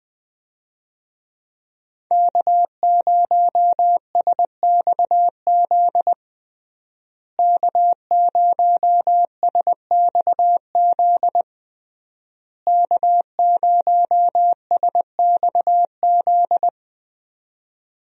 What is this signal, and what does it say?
Morse code, heard as a keyed tone, about 20 wpm, 700 Hz. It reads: K0SXZ K0SXZ K0SXZ
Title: USA Call Signs - 5 Characters - Repeated 3x - 20wpm